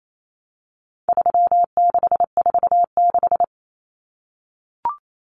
Morse code, heard as a keyed tone, about 28 wpm, 700 Hz. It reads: 3646
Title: Common Cut Numbers - 4-Digit 28wpm